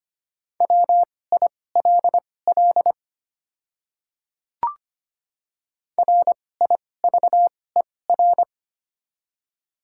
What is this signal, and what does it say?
Morse code, heard as a keyed tone, about 25 wpm, spaced as for 18 wpm, 700 Hz. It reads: WILL RIVER
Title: 2x Character Spacing - Top 300 Words - 25wpm